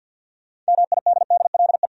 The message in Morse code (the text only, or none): MIDDLE